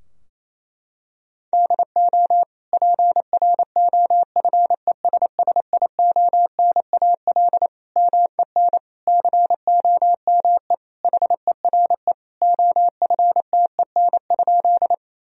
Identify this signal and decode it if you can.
Morse code, heard as a keyed tone, about 28 wpm, 700 Hz. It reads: DO PROFESSIONAL MEN COME HERE OFTEN?
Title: Sentences from Top 1000 Words - Review Part 4 - 28wpm